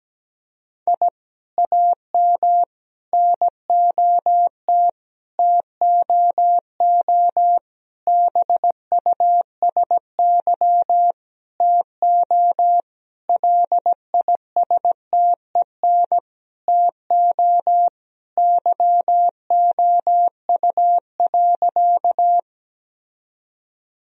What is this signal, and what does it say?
Morse code, heard as a keyed tone, about 17 wpm, 700 Hz. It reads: I AM NOT TOO BUSY TO LISTEN TO YOU.
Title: Sentences from Top 1000 Words - Review Part 3 - 17wpm